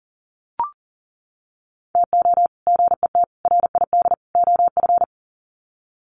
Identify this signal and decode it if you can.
Morse code, heard as a keyed tone, about 40 wpm, 700 Hz. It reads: TO GET RID OF
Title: Sets of 4 Words 40wpm